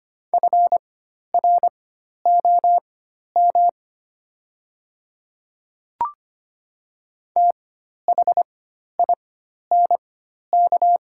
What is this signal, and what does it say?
Morse code, heard as a keyed tone, about 25 wpm, spaced as for 12 wpm, 700 Hz. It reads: FROM THINK
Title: 4x Character Spacing - Top 200 Words - 25wpm